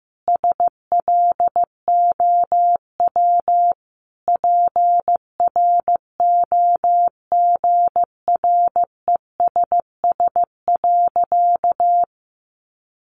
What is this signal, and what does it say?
Morse code, heard as a keyed tone, about 15 wpm, 700 Hz. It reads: SLOW PROGRESS.
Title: Sentences from Top 1400 Words 15wpm